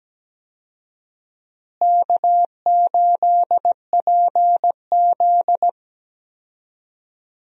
Morse code, heard as a keyed tone, about 17 wpm, 700 Hz. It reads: K8PZ
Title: USA Call Signs - 4 Characters - Repeated 3x - 17wpm